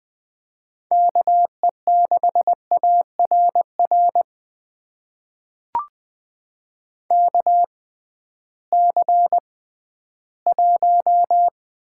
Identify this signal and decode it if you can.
Morse code, heard as a keyed tone, about 20 wpm, spaced as for 13 wpm, 700 Hz. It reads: KE6ARR KC1
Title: USA Call Signs - 6 Characters - 6x Character Spacing - 20wpm